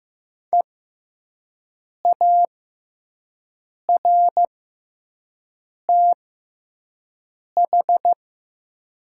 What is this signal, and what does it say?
Morse code, heard as a keyed tone, about 15 wpm, spaced as for 5 wpm, 700 Hz. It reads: EARTH
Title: 6x Character Spacing - Top 200 Words - 15wpm